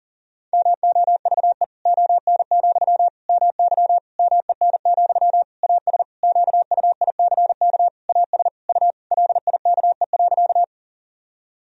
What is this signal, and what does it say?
Morse code, heard as a keyed tone, about 40 wpm, 700 Hz. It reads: MOVE ON, MY MEN, AS QUICK AS U LIKE.
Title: Sentences from Top 500 Words and Abbreviations 40wpm